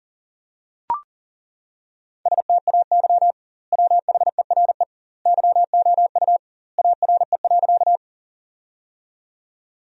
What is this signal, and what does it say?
Morse code, heard as a keyed tone, about 40 wpm, 700 Hz. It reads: STAY WHERE YOU ARE.
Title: 2x Word-Spacing - Sentences from Top 500 Words - 40wpm